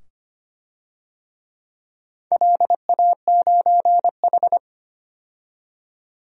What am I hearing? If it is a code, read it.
Morse code, heard as a keyed tone, about 25 wpm, 700 Hz. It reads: LA9H
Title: DXCC-100 Call Signs - 4 Characters - 25wpm